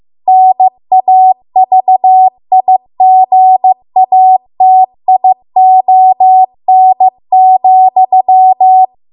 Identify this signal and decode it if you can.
Morse code, heard as a keyed tone, about 15 wpm, 750 Hz. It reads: NAVIGATION,
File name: Archive / 15 WPM / 15WPM